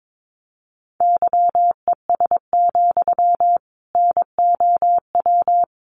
Morse code, heard as a keyed tone, about 22 wpm, 700 Hz. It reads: YES, NOW
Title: Sentences from Top 800 Words 22wpm